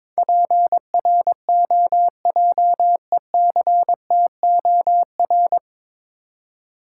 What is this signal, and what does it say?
Morse code, heard as a keyed tone, about 22 wpm, 690 Hz. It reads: PROJECTOR
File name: Easy Words - Speed-Racing with Hints - 22wpm